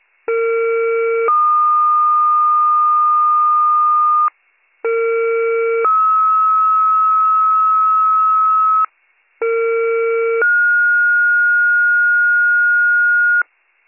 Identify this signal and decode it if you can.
Two-tone page, heard as a one-second tone, then a three-second tone, about 460 then 1220 Hz, about 460 then 1290 Hz, about 460 then 1430 Hz. It is heard from 0.3 to 13.4 s.